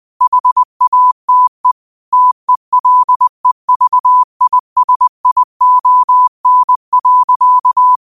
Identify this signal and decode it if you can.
Morse code, heard as a keyed tone, about 20 wpm, 1000 Hz. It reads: HATE TELEVISION.